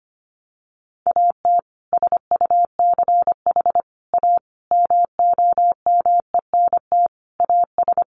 Morse code, heard as a keyed tone, about 25 wpm, 700 Hz. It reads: AT SUCH A MOMENT AS
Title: Sentences from Top 1000 Words 25wpm